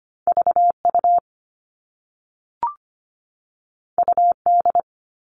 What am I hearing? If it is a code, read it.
Morse code, heard as a keyed tone, about 25 wpm, 700 Hz. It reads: V U U D
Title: All Cut Numbers - 2-Digit 25wpm